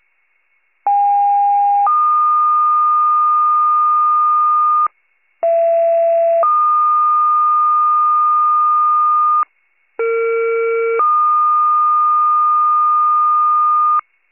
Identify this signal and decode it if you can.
Two-tone page, heard as a one-second tone, then a three-second tone, about 810 then 1220 Hz, about 670 then 1220 Hz, about 460 then 1220 Hz. It is heard from 0.9 to 14.0 s.
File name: pager / 41 RESCUE / wcalert